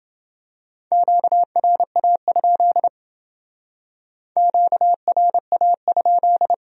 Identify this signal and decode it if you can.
Morse code, heard as a keyed tone, about 28 wpm, 700 Hz. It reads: QRA? QRA?